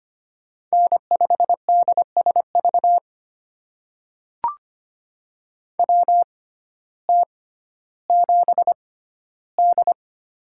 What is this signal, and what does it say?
Morse code, heard as a keyed tone, about 25 wpm, spaced as for 14 wpm, 700 Hz. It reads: N5DSV WT7D
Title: USA Call Signs - 6x Character Spacing - 25wpm